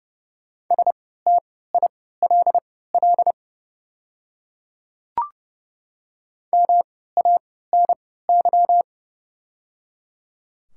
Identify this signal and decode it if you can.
Morse code, heard as a keyed tone, about 30 wpm, spaced as for 17 wpm, 700 Hz. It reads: STILL MANY
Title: 3x Character Spacing - Top 300 Words - 30wpm